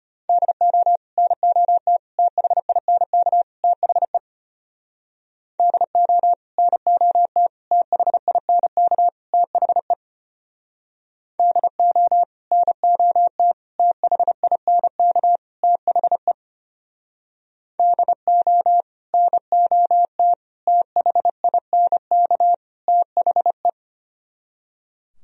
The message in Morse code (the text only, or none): DO NOT THINK THE DO NOT THINK THE DO NOT THINK THE DO NOT THINK THE